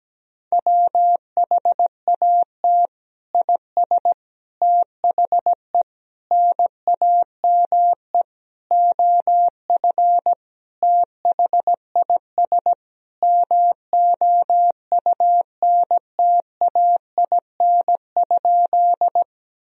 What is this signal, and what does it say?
Morse code, heard as a keyed tone, about 17 wpm, 700 Hz. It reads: WHAT IS THE NAME OF THIS MOUNTAIN?
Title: Sentences from Top 1000 Words - Review Part 1 - 17wpm